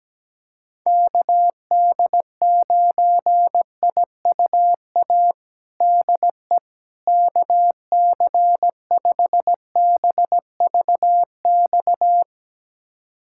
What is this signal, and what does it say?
Morse code, heard as a keyed tone, about 17 wpm, 700 Hz. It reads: KD9IUA DE KC5BVX
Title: USA Callsign DE Callsign 17wpm